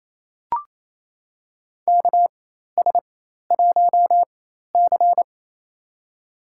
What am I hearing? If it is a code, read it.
Morse code, heard as a keyed tone, about 28 wpm, spaced as for 13 wpm, 700 Hz. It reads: KS1C
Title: USA Call Signs - 4 Characters - 4x Character Spacing - 28wpm